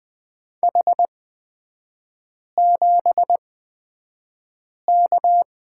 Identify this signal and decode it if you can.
Morse code, heard as a keyed tone, about 20 wpm, spaced as for 5 wpm, 700 Hz. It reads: H7K
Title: 224 Lesson 36e - BT - Three Characters 20wpm